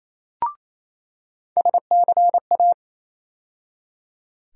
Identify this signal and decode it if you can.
Morse code, heard as a keyed tone, about 28 wpm, 700 Hz. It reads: SCA
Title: USA Call Sign Postfixes - 3 Letters 28wpm